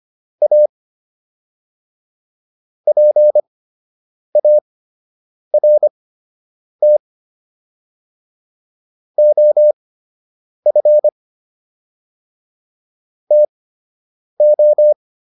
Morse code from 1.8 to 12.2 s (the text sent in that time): PART OF